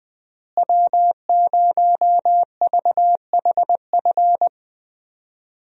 Morse code, heard as a keyed tone, about 21 wpm, 700 Hz. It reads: W0VHF
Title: USA Call Signs - 5 Characters - Repeated 3x - 20wpm